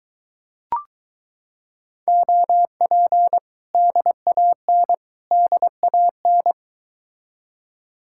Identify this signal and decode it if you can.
Morse code, heard as a keyed tone, about 22 wpm, 700 Hz. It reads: OP DAN DAN